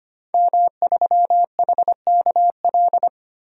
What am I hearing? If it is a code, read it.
Morse code, heard as a keyed tone, about 25 wpm, 700 Hz. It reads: M3HKL
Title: DXCC-100 Call Signs - 5 Characters - Repeated 2x - 25wpm